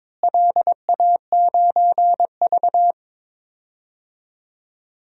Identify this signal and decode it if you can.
Morse code, heard as a keyed tone, about 21 wpm, 700 Hz. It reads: LA9V
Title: DXCC-100 Call Signs - 4 Characters - Repeated 3x - 22wpm